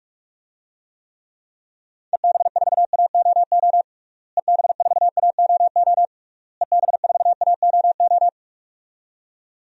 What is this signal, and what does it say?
Morse code, heard as a keyed tone, about 46 wpm, 700 Hz. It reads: EB4AOO EB4AOO EB4AOO
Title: DXCC-100 Call Signs - 6 Characters - Repeated 3x - 45wpm